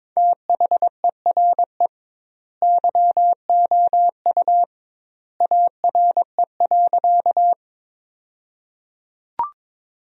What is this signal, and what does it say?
Morse code, heard as a keyed tone, about 22 wpm, 700 Hz. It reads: THERE YOU ARE.
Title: 2x Word-Spacing - Sentences from Top 100 Words - 22wpm